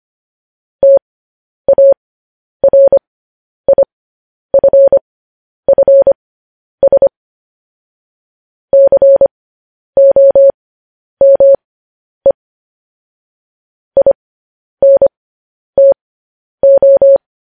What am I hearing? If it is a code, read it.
Morse code, heard as a keyed tone, about 25 wpm, spaced as for 10 wpm, 550 Hz. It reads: TARIFFS COME INTO
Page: news headlines in morse code at 10 wpm